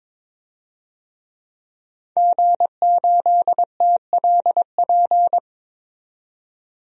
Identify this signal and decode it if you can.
Morse code, heard as a keyed tone, about 22 wpm, 700 Hz. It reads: G8TLP
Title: DXCC-100 Call Signs - 5 Characters - 22wpm